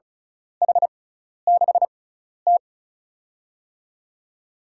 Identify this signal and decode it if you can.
Morse code, heard as a keyed tone, about 34 wpm, spaced as for 12 wpm, 700 Hz. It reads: H6T
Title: USA Call Signs - 4 Characters - 6x Character Spacing - 35wpm